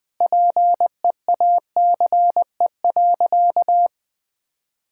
Morse code, heard as a keyed tone, about 20 wpm, 700 Hz. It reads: PEACE.
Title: Sentences with 9-Word Phrases - 20wpm